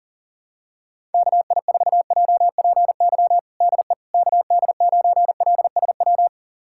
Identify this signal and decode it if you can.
Morse code, heard as a keyed tone, about 40 wpm, 700 Hz. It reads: KI4JPY DE KD9LSW
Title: USA Callsign DE Callsign 40wpm